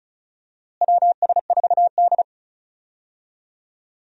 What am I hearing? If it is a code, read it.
Morse code, heard as a keyed tone, about 36 wpm, 700 Hz. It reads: WS4D
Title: USA Call Signs - 4 Characters - Repeated 2x - 35wpm